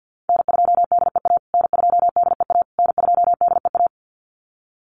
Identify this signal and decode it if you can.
Morse code, heard as a keyed tone, about 50 wpm, 700 Hz. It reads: N2DEA N2DEA N2DEA
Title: USA Call Signs Repeated 3x 50wpm